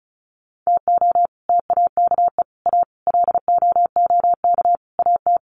Morse code, heard as a keyed tone, about 34 wpm, 700 Hz. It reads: TO TAKE A LOOK AT
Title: Sets of 5 Words 35wpm